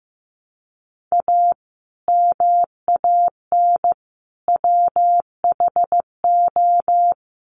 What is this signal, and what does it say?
Morse code, heard as a keyed tone, about 15 wpm, 700 Hz. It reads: A MAN WHO
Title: Sets of 3 Words 15wpm